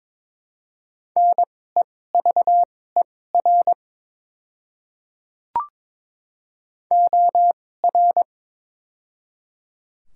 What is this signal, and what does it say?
Morse code, heard as a keyed tone, about 22 wpm, spaced as for 16 wpm, 700 Hz. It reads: NEVER OR
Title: 2x Character Spacing - Top 500 Words - 22wpm